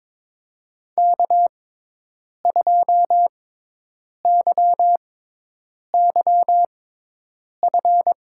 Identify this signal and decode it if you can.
Morse code, heard as a keyed tone, about 22 wpm, spaced as for 8 wpm, 700 Hz. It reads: K2YYF
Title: USA Call Signs - 5 Characters - 6x Character Spacing - 22wpm